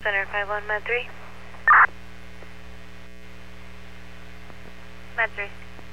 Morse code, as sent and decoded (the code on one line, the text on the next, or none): none